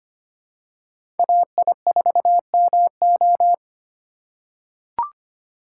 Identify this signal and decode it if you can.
Morse code, heard as a keyed tone, about 25 wpm, 700 Hz. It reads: AI4MO
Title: USA Call Signs - 5 Characters - 2x Character Spacing - 25wpm